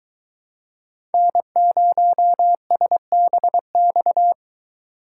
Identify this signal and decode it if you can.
Morse code, heard as a keyed tone, about 22 wpm, 700 Hz. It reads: N0SBX